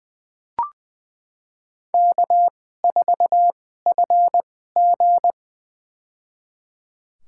Morse code, heard as a keyed tone, about 20 wpm, spaced as for 15 wpm, 700 Hz. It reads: K4FG
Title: USA Call Signs - 4 Characters - 2x Character Spacing - 20wpm